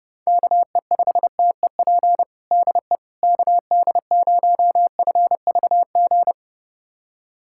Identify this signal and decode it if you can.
Morse code, heard as a keyed tone, about 30 wpm, 690 Hz. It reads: KE5TEP DE KD0FVG
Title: USA Callsign DE Callsign - 6 Characters - 30wpm